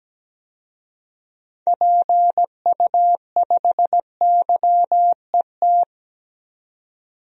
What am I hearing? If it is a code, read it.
Morse code, heard as a keyed tone, about 17 wpm, 700 Hz. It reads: PU5YET
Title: DXCC-100 Call Signs - 6 Characters - 17wpm